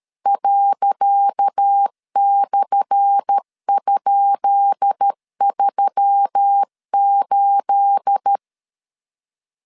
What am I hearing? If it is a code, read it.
Morse code, heard as a keyed tone, about 13 wpm, 800 Hz. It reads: ./?38
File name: LEZIONI / 10 SIGNS / 10C DOT 12wpm 2451